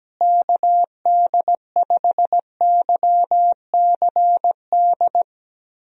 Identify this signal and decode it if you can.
Morse code, heard as a keyed tone, about 17 wpm, 700 Hz. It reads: KD5YCD
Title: USA Call Signs - 6 Characters - Repeated 2x - 17wpm